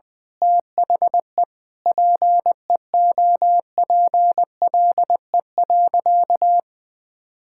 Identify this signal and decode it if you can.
Morse code, heard as a keyed tone, about 20 wpm, 700 Hz. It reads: THE PEOPLE.
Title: Sentences with 9-Word Phrases - Spoken Beginning - 20wpm